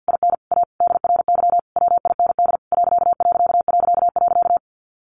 Morse code, heard as a keyed tone, about 50 wpm, 700 Hz. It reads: IN A DRY WIND ....
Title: A Sound Of Thunder 50wpm